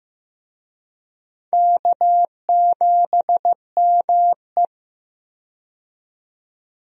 Morse code, heard as a keyed tone, about 16 wpm, 700 Hz. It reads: K7ME
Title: USA Call Signs - 4 Characters - 15wpm